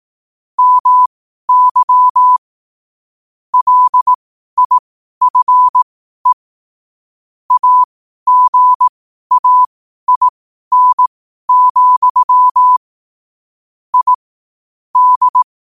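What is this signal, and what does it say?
Morse code, heard as a keyed tone, about 18 wpm, spaced as for 12 wpm, 1000 Hz. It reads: MY LIFE AGAIN, I D
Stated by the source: Morse code sent at 12 words per minute.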